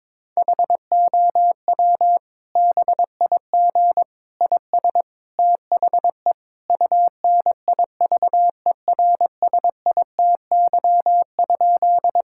Morse code, heard as a keyed tone, about 22 wpm, 700 Hz. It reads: HOW BIG IS THE UNIVERSITY?
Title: Sentences from Top 1000 Words - Review Part 5 - 22wpm